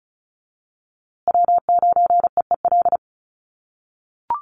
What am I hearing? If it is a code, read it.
Morse code, heard as a keyed tone, about 34 wpm, 700 Hz. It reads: W9EEL
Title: USA Call Signs Repeated 2x 35wpm